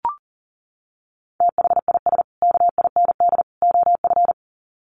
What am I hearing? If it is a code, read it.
Morse code, heard as a keyed tone, about 40 wpm, 700 Hz. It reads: THIS KIND OF
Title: Sets of 3 Words 40wpm